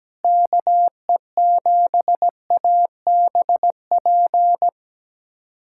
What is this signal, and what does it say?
Morse code, heard as a keyed tone, about 17 wpm, 700 Hz. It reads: KE7ABP
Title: USA Callsign DE Callsign - 6 Characters - 17wpm